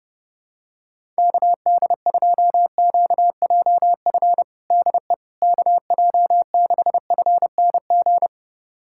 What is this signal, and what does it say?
Morse code, heard as a keyed tone, about 30 wpm, 700 Hz. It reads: KD2QJF DE KJ6FNG
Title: USA Callsign DE Callsign - 6 Characters - 30wpm